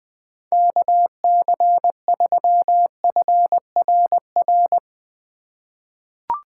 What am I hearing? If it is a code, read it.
Morse code, heard as a keyed tone, about 20 wpm, 700 Hz. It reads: KC3FRR
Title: USA Call Signs - 6 Characters - 3x Character Spacing - 20wpm